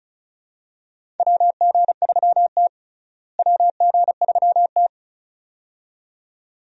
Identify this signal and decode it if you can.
Morse code, heard as a keyed tone, about 34 wpm, 700 Hz. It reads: WG3T WG3T
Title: USA Call Signs - 4 Characters - Repeated 2x - 35wpm